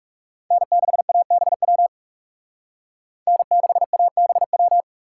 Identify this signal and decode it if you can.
Morse code, heard as a keyed tone, about 38 wpm, 700 Hz. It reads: N6ABW N6ABW